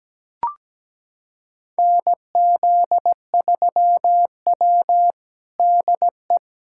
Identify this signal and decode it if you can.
Morse code, heard as a keyed tone, about 17 wpm, 700 Hz. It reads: NZ3W DE
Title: USA Callsign DE Callsign - 4 Characters - 17wpm